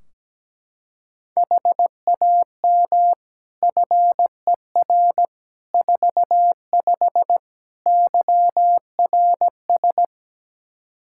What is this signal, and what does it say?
Morse code, heard as a keyed tone, about 17 wpm, 700 Hz. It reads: HAM FER 45 YRS
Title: Practice QSO - Exchange 3 - More Info - 17wpm